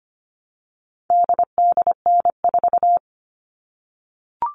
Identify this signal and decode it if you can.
Morse code, heard as a keyed tone, about 25 wpm, 700 Hz. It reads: DDN4
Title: All Cut Numbers - 4-Digit 25wpm